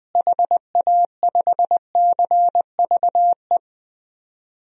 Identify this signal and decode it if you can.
Morse code, heard as a keyed tone, about 20 wpm, 700 Hz. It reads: HA5CVE
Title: DXCC-100 Call Signs - 6 Characters - Repeated 2x - 20wpm